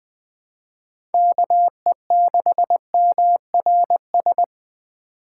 Morse code, heard as a keyed tone, about 20 wpm, 700 Hz. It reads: KE6MRS
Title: USA Call Signs - 6 Characters - 3x Character Spacing - 20wpm